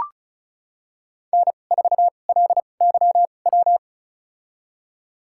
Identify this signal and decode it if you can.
Morse code, heard as a keyed tone, about 34 wpm, spaced as for 25 wpm, 700 Hz. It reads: N4LYW
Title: USA Call Signs - 5 Characters - 2x Character Spacing - 35wpm